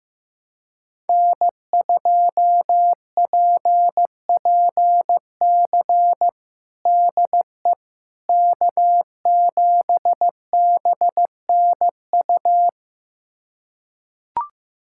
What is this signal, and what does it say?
Morse code, heard as a keyed tone, about 16 wpm, 700 Hz. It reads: N2PPC DE K7BNU
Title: USA Callsign DE Callsign - 5 Characters - 15wpm